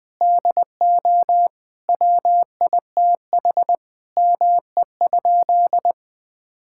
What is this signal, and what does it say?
Morse code, heard as a keyed tone, about 21 wpm, 700 Hz. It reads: DO WITH ME?
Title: Sentences with 5-Word Phrases - Spoken Beginning - 20wpm